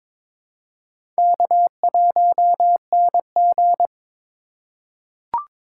Morse code, heard as a keyed tone, about 22 wpm, 700 Hz. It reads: K1NG
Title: USA Call Signs - 4 Characters - 2x Character Spacing - 22wpm